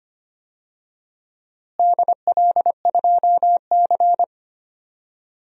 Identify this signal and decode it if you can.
Morse code, heard as a keyed tone, about 25 wpm, 700 Hz. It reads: DL2C
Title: DXCC-100 Call Signs - 4 Characters - Repeated 2x - 25wpm